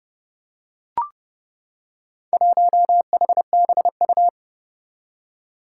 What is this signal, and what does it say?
Morse code, heard as a keyed tone, about 30 wpm, 700 Hz. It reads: 1HBU
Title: Four Challenging Letter-Number Combinations 30wpm